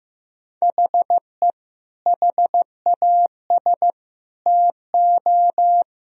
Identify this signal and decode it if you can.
Morse code, heard as a keyed tone, about 15 wpm, 700 Hz. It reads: HE HAS TO
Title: Sentences from Top 1000 Words - Review Part 2 - 15wpm